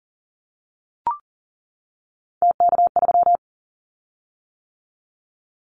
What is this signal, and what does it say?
Morse code, heard as a keyed tone, about 39 wpm, 700 Hz. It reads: TK3
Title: Callsign Prefixes for Top 100 DXCC - 40wpm